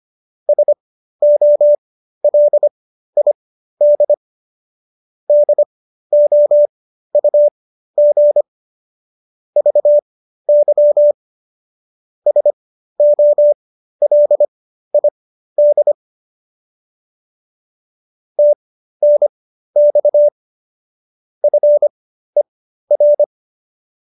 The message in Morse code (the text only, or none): SOLID DOUG VY SOLID TNX FER